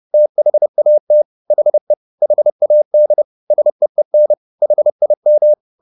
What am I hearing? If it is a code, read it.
Morse code, heard as a keyed tone, about 30 wpm, 600 Hz. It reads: THAT HE HAD SEEN HIM
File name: the old man and the sea / the old man and the sea 1 30wpm